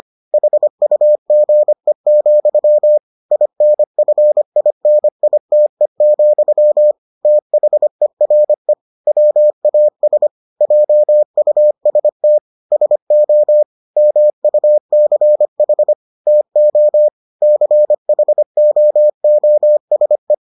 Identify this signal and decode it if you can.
Morse code, heard as a keyed tone, about 25 wpm, 600 Hz. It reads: HUGE, INFINITE, THERE WAS JUST SO MUCH TO CHOOSE